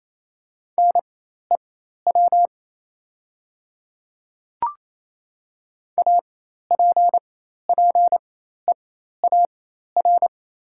Morse code, heard as a keyed tone, about 28 wpm, spaced as for 13 wpm, 690 Hz. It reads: NEW APPEAR
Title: 4x Character Spacing - Top 500 Words - 28wpm